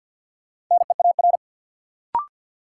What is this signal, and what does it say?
Morse code, heard as a keyed tone, about 50 wpm, 700 Hz. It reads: NEAR
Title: Sentences from Top 200 Words - New Word Drill - 50wpm